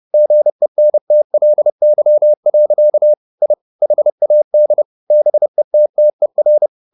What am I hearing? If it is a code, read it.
Morse code, heard as a keyed tone, about 30 wpm, 600 Hz. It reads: GENTLY. I HAD BETTER
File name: the old man and the sea / the old man and the sea 1 30wpm